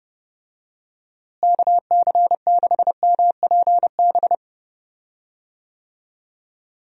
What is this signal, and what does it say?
Morse code, heard as a keyed tone, about 31 wpm, 700 Hz. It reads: KC6MPB
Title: USA Call Signs - 6 Characters - Repeated 3x - 30wpm